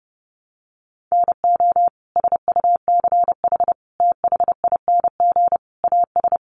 Morse code, heard as a keyed tone, about 30 wpm, 700 Hz. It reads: NO SUCH THING AS
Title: Sets of 5 Words Encore 30wpm